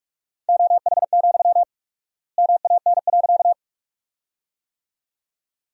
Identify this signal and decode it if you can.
Morse code, heard as a keyed tone, about 45 wpm, 700 Hz. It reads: OH, MAN.
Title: 4x Word-Spacing - Sentences from Top 500 Words - 45wpm